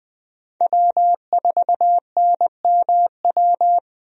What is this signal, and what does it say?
Morse code, heard as a keyed tone, about 21 wpm, 700 Hz. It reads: W4NMW
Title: USA Call Signs - 5 Characters - Repeated 3x - 20wpm